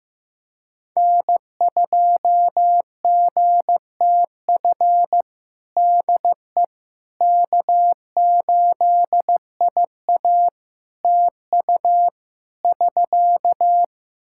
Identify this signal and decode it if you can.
Morse code, heard as a keyed tone, about 15 wpm, 700 Hz. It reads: N2GTF DE K8IA TU <SK>
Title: Practice QSO - Exchange 4 - Ending - 15wpm